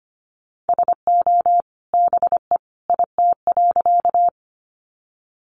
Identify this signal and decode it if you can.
Morse code, heard as a keyed tone, about 25 wpm, 700 Hz. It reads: SO BE IT.
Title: Sentences from Top 100 Words 25wpm